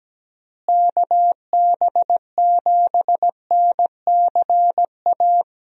Morse code, heard as a keyed tone, about 17 wpm, 700 Hz. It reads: KB7NCA
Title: USA Call Signs - 6 Characters - Repeated 2x - 17wpm